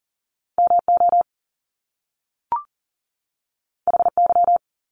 Morse code, heard as a keyed tone, about 39 wpm, 700 Hz. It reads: MO HY
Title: Top Two Letter Combinations in Words 40wpm